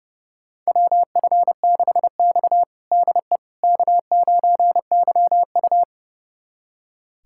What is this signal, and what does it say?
Morse code, heard as a keyed tone, about 30 wpm, 700 Hz. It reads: WF6X DE K9YU
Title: USA Callsign DE Callsign - 4 Characters - 30wpm